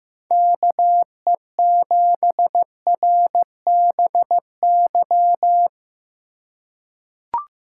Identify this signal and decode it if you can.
Morse code, heard as a keyed tone, about 15 wpm, 700 Hz. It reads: KE7RBY
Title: USA Call Signs - 6 Characters - 4x Character Spacing - 15wpm